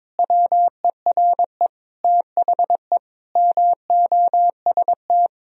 Morse code, heard as a keyed tone, about 21 wpm, 700 Hz. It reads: WERE THE MOST
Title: Sentences from Top 1000 Words - Review Part 3 - 22wpm